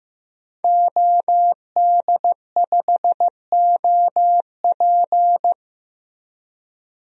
Morse code, heard as a keyed tone, about 15 wpm, 700 Hz. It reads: OD5OP
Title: DXCC-100 Call Signs - 5 Characters - Repeated 2x - 15wpm